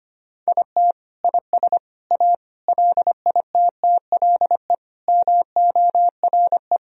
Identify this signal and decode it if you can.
Morse code, heard as a keyed tone, about 25 wpm, 700 Hz. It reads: IT IS A LITTLE MORE
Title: Sentences from Top 1000 Words - Review Part 1 - 25wpm